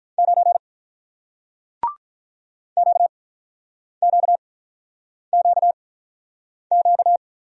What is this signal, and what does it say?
Morse code, heard as a keyed tone, about 46 wpm, spaced as for 9 wpm, 700 Hz. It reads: 9 QQQQ